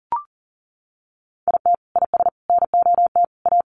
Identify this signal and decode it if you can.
Morse code, heard as a keyed tone, about 40 wpm, 700 Hz. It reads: IT IS NOT A
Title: Sentences from Top 800 Words 40wpm